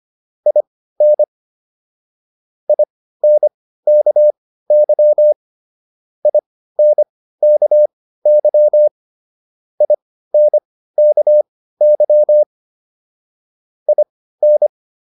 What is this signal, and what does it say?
Morse code, heard as a keyed tone, about 25 wpm, spaced as for 15 wpm, 600 Hz. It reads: IN INKY INKY INKY IN